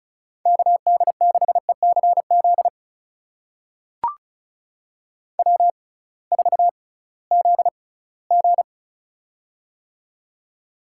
Morse code, heard as a keyed tone, about 36 wpm, spaced as for 21 wpm, 700 Hz. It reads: KD6ECZ W4ZG
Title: USA Call Signs - 6x Character Spacing - 35wpm